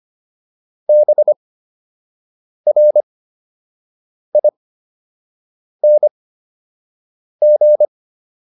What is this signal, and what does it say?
Morse code, heard as a keyed tone, about 25 wpm, spaced as for 6 wpm, 600 Hz. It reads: BRING